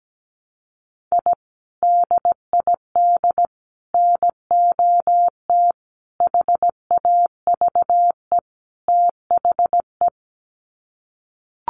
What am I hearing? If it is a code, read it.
Morse code, heard as a keyed tone, about 17 wpm, 700 Hz. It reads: I DID NOT HAVE THE
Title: Sets of 5 Words Encore 17wpm